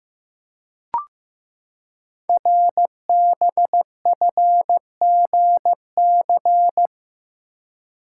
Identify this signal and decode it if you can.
Morse code, heard as a keyed tone, about 15 wpm, 700 Hz. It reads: RBFGC
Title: Code Groups - 5 Letters - 15wpm